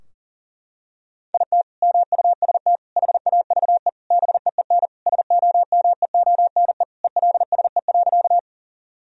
Morse code, heard as a keyed tone, about 40 wpm, 700 Hz. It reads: IT MUST HAVE BEEN SOMEONE ELSE.
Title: Sentences from Top 1000 Words - Review Part 4 - 40wpm